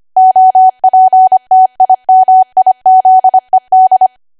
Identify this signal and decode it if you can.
Morse code, heard as a keyed tone, about 25 wpm, 750 Hz. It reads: OPTIMIZED
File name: cwa / W1AW-Level3 / 25WPM